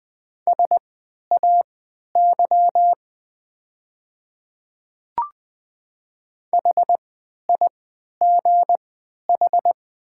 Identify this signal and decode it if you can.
Morse code, heard as a keyed tone, about 20 wpm, spaced as for 11 wpm, 690 Hz. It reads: SAY HIGH
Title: 3x Character Spacing - Top 200 Words - 20wpm